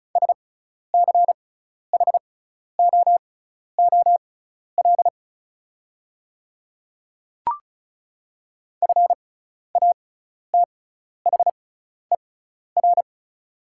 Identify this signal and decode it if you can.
Morse code, heard as a keyed tone, about 36 wpm, spaced as for 12 wpm, 690 Hz. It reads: SCHOOL FATHER
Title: 6x Character Spacing - Top 300 Words - 35wpm